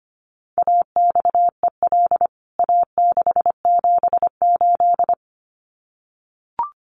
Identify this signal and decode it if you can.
Morse code, heard as a keyed tone, about 25 wpm, 700 Hz. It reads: AXEL A678
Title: CWT Exchange 25wpm